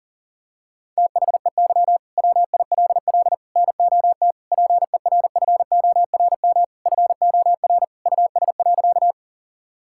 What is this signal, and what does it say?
Morse code, heard as a keyed tone, about 40 wpm, 700 Hz. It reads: THEY WILL NOT PERFORM FOR US.
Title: Sentences from Top 1000 Words - Review Part 5 - 40wpm